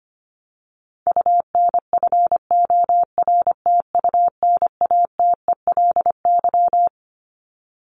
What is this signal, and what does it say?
Morse code, heard as a keyed tone, about 25 wpm, 700 Hz. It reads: UNFORTUNATELY
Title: Top 1751-2000 Words 25wpm